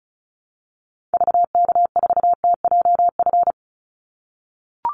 Morse code, heard as a keyed tone, about 36 wpm, 700 Hz. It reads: VK4TJF
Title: Call Signs - 35wpm